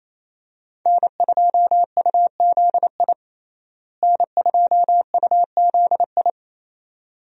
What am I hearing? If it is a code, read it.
Morse code, heard as a keyed tone, about 28 wpm, 700 Hz. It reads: N2UZI N2UZI
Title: USA Call Signs - 5 Characters - Repeated 2x - 28wpm